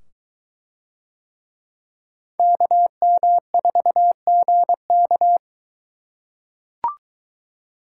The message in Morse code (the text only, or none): KM4GK